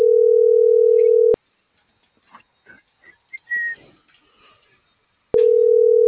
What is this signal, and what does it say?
Ringback tone (North America), 440 + 480 Hz.